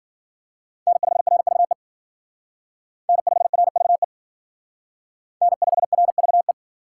Morse code, heard as a keyed tone, about 51 wpm, 700 Hz. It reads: N5RVE N5RVE N5RVE